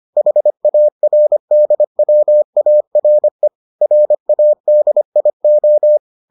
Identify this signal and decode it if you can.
Morse code, heard as a keyed tone, about 25 wpm, 600 Hz. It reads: HARDWARE RADIO